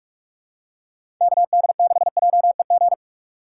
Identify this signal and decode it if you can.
Morse code, heard as a keyed tone, about 44 wpm, 700 Hz. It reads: KD6JEG
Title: USA Call Signs - 6 Characters - 4x Character Spacing - 45wpm